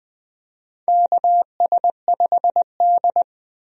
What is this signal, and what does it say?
Morse code, heard as a keyed tone, about 20 wpm, 700 Hz. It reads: KS5D
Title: USA Call Signs - 4 Characters - Repeated 3x - 20wpm